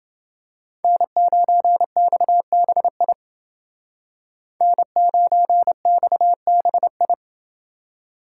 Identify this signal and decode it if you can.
Morse code, heard as a keyed tone, about 29 wpm, 700 Hz. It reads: N9XBI N9XBI